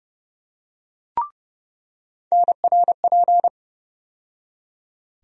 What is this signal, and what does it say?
Morse code, heard as a keyed tone, about 30 wpm, 700 Hz. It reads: NRP
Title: USA Call Sign Postfixes - 3 Letters 30wpm